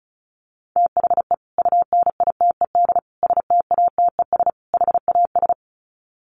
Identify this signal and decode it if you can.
Morse code, heard as a keyed tone, about 34 wpm, 700 Hz. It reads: THE UNITED STATES HAS
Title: Sets of 4 Words 35wpm